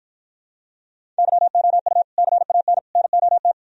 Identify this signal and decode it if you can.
Morse code, heard as a keyed tone, about 53 wpm, 700 Hz. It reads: YOU CAN NOT